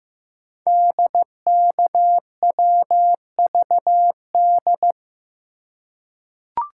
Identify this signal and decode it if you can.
Morse code, heard as a keyed tone, about 15 wpm, 700 Hz. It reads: DKWVD
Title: Code Groups - 5 Letters - 15wpm